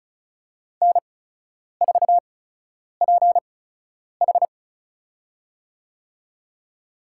Morse code, heard as a keyed tone, about 36 wpm, spaced as for 10 wpm, 700 Hz. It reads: N4PH
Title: USA Call Signs - 4 Characters - 8x Character Spacing - 35wpm